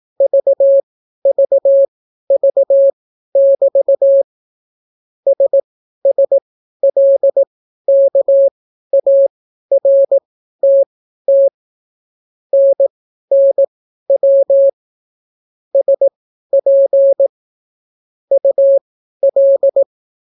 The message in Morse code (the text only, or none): VVV<BT> SSLKARTT NNW SP UL